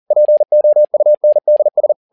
Morse code, heard as a keyed tone, about 40 wpm, 600 Hz. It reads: POUNDS
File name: learning center / tower trouble / tower touble 40wpm